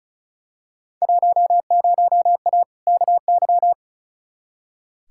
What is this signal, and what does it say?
Morse code, heard as a keyed tone, about 36 wpm, 700 Hz. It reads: 10A KY
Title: Field Day Exchanges Encore - 35wpm